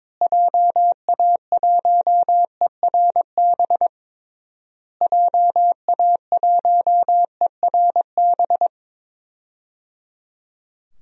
Morse code, heard as a keyed tone, about 22 wpm, 700 Hz. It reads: JA1ERB JA1ERB
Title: DXCC-100 Call Signs - 6 Characters - Repeated 3x - 22wpm